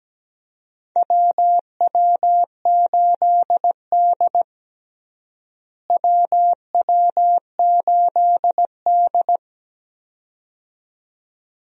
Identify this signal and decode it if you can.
Morse code, heard as a keyed tone, about 17 wpm, 700 Hz. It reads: WW8D WW8D
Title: USA Call Signs - 4 Characters - Repeated 2x - 17wpm